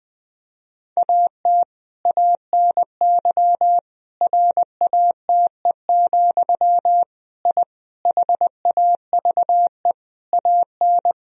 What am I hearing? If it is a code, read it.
Morse code, heard as a keyed tone, about 20 wpm, 700 Hz. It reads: AT ANY RATE, I HAVE AN
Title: Sentences from Top 1000 Words - Review Part 4 - 20wpm